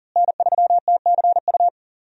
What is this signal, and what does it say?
Morse code, heard as a keyed tone, about 37 wpm, 700 Hz. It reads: N3TCU
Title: USA Call Signs - 5 Characters - Repeated 2x - 40wpm